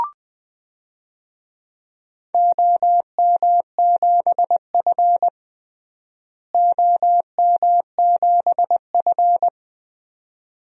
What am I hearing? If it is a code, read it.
Morse code, heard as a keyed tone, about 20 wpm, 700 Hz. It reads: OM7F OM7F
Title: DXCC-100 Call Signs - 4 Characters - Repeated 2x - 20wpm